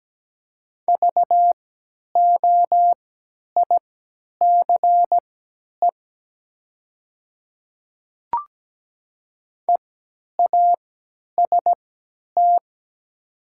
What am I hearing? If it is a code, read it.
Morse code, heard as a keyed tone, about 17 wpm, spaced as for 10 wpm, 700 Hz. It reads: VOICE EAST
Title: 3x Character Spacing - Top 500 Words - 17wpm